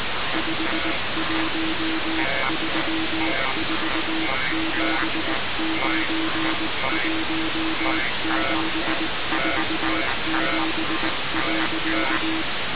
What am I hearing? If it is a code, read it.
Morse code, heard as a keyed tone, about 19 wpm, 320 Hz, with noise. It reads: 5134791747.
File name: radiocap / REC z